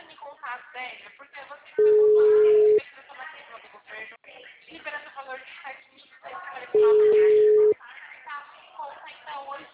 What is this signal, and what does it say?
Ringback tone (Europe, 425 Hz), with speech.